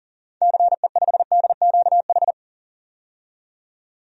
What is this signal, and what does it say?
Morse code, heard as a keyed tone, about 40 wpm, 700 Hz. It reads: CE5DQH
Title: DXCC-100 Call Signs - 6 Characters - 40wpm